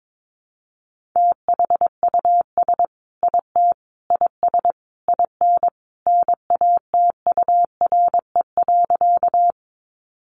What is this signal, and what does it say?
Morse code, heard as a keyed tone, about 22 wpm, 700 Hz. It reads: THUS IT IS IN NATURE.
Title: Sentences from Top 800 Words 22wpm